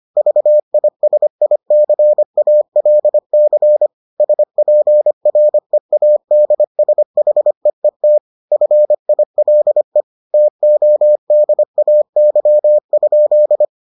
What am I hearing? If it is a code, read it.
Morse code, heard as a keyed tone, about 25 wpm, 600 Hz. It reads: VISICALC SPREADSHEET FILE TODAY?